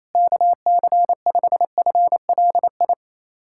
Morse code, heard as a keyed tone, about 28 wpm, 700 Hz. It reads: KC5FLI
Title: USA Callsign DE Callsign - 6 Characters - 28wpm